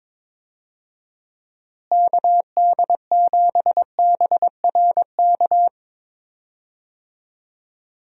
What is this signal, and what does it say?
Morse code, heard as a keyed tone, about 22 wpm, 700 Hz. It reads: KD7BRK
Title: USA Call Signs - 6 Characters - Repeated 3x - 22wpm